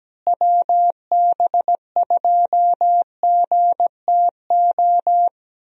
Morse code, heard as a keyed tone, about 17 wpm, 700 Hz. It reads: WB2GTO
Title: USA Call Signs - 6 Characters - Repeated 3x - 17wpm